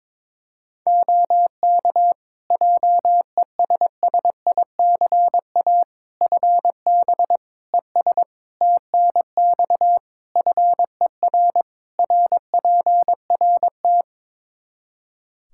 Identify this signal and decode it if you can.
Morse code, heard as a keyed tone, about 22 wpm, 700 Hz. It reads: OK JESSICA FB ES TNX FER RPRT
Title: Practice QSO - Exchange 2 - Info - 22wpm